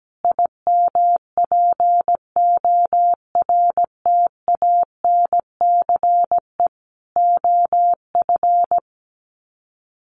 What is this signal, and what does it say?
Morse code, heard as a keyed tone, about 17 wpm, 700 Hz. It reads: IMPORTANCE OF
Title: Sets of 3 Words 17wpm